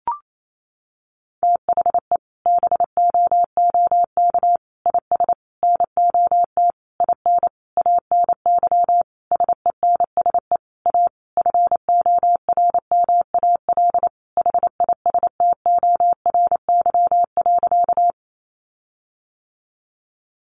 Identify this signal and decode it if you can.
Morse code, heard as a keyed tone, about 28 wpm, 700 Hz. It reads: THE BOOK IS NOT IN ANY SENSE A FORMAL HISTORY.
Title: Sentences from Top 1300 Words 28wpm